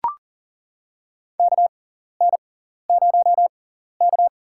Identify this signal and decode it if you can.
Morse code, heard as a keyed tone, about 40 wpm, spaced as for 14 wpm, 700 Hz. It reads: KN0K
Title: USA Call Signs - 4 Characters - 6x Character Spacing - 40wpm